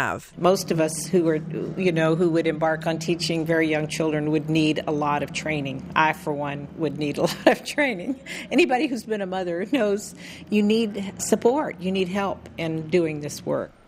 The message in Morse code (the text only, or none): none